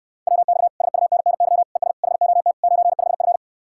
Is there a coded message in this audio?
Yes: Morse code from 0.3 to 3.4 s, heard as a keyed tone, about 69 wpm, 700 Hz. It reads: WX SUNNY ES HOT 95F